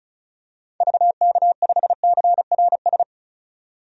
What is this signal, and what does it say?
Morse code, heard as a keyed tone, about 36 wpm, 700 Hz. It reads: VK5CRS
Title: DXCC-100 Call Signs - 6 Characters - 35wpm